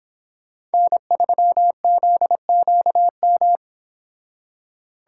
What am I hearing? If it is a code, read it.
Morse code, heard as a keyed tone, about 30 wpm, 700 Hz. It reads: N3ZQM